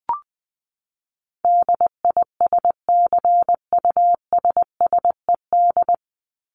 Morse code, heard as a keyed tone, about 20 wpm, 700 Hz. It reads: DISCUSSED
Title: ED Suffix - Top 100 Words with Base Words 20wpm